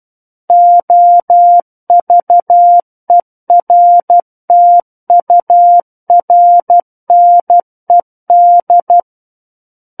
Morse code, heard as a keyed tone, about 12 wpm, 700 Hz. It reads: OVERTURNED